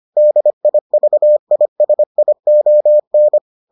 Morse code, heard as a keyed tone, about 25 wpm, 600 Hz. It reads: DIVISION